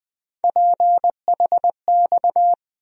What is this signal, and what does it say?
Morse code, heard as a keyed tone, about 20 wpm, 700 Hz. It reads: PHX
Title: USA Call Sign Postfixes - 3 Letters 20wpm